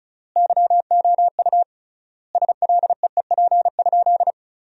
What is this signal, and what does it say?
Morse code, heard as a keyed tone, about 34 wpm, 700 Hz. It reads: YOU SLEEP?
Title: 3x Word-Spacing - Sentences from Top 500 Words - 35wpm